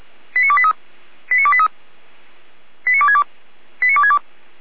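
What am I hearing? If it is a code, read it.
Five-tone selective call (ZVEI), heard as quick runs of five tones, tones of about 70 ms. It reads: 89283 89283 89262 89262
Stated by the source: ZVEI five-tone call 89273 received via SDR.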